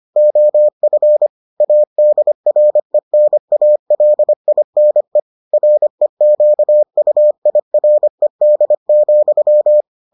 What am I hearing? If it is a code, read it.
Morse code, heard as a keyed tone, about 25 wpm, 600 Hz. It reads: OF ADRENALINE REQUIRED,